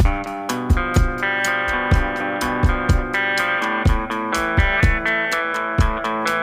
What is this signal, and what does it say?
Emergency Alert System (SAME) tones (header bursts), heard as harsh buzzing bursts. It is heard from 1.1 to 5.7 s.